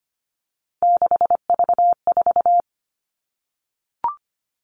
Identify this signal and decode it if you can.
Morse code, heard as a keyed tone, about 25 wpm, 700 Hz. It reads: 6V4
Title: All Cut Numbers - 3-Digit 25wpm